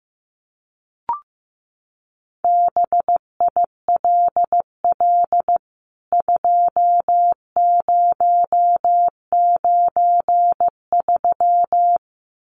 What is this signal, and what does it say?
Morse code, heard as a keyed tone, about 15 wpm, 700 Hz. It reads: BILL 2093
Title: CWT Exchange 15wpm